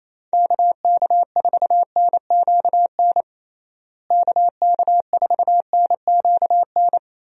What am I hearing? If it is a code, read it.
Morse code, heard as a keyed tone, about 28 wpm, 700 Hz. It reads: KK4NQN KK4NQN
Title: USA Call Signs - 6 Characters - Repeated 3x - 28wpm